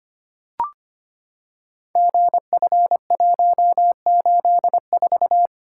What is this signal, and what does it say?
Morse code, heard as a keyed tone, about 25 wpm, 700 Hz. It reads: GF184
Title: Code Groups - 5 Letter:Numbers - 25wpm